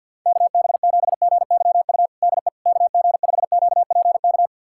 Morse code, heard as a keyed tone, about 52 wpm, 700 Hz. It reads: KB7GYU DE KG5QPK
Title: USA Callsign DE Callsign - 6 Characters - 50wpm